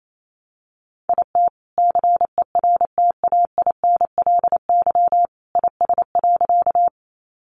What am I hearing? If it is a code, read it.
Morse code, heard as a keyed tone, about 28 wpm, 700 Hz. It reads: IT CERTAINLY IS.
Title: Sentences from Top 900 Words 28wpm